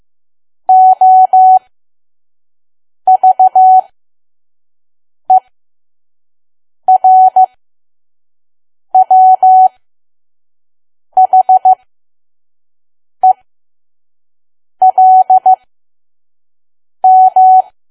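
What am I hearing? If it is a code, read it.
Morse code, heard as a keyed tone, about 15 wpm, spaced as for 5 wpm, 750 Hz. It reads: OVERWHELM
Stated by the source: Morse code practice at 5 WPM from KQ2P.COM.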